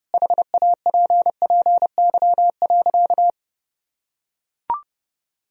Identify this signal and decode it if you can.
Morse code, heard as a keyed tone, about 30 wpm, 700 Hz. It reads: HAPPY.
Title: Sentences from Top 1000 Words - Review Part 1 - 30wpm